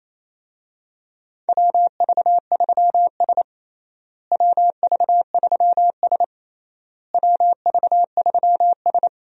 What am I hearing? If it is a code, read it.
Morse code, heard as a keyed tone, about 28 wpm, 700 Hz. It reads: WV3S WV3S WV3S
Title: USA Call Signs - 4 Characters - Repeated 3x - 28wpm